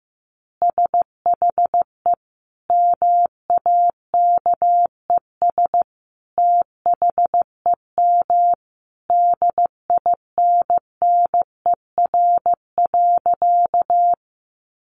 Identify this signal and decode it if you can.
Morse code, heard as a keyed tone, about 15 wpm, 700 Hz. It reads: SHE MAKES THEM DINNER.
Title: Easy Phrases 15wpm